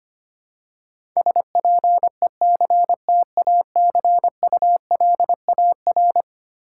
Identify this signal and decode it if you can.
Morse code, heard as a keyed tone, about 25 wpm, 700 Hz. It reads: SPECTACULAR
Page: morse-code-ninja-practice-25wpm